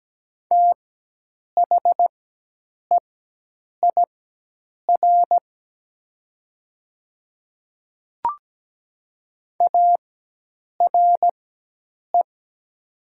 Morse code, heard as a keyed tone, about 17 wpm, spaced as for 8 wpm, 700 Hz. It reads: THEIR ARE
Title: 4x Character Spacing - Top 100 Words - 17wpm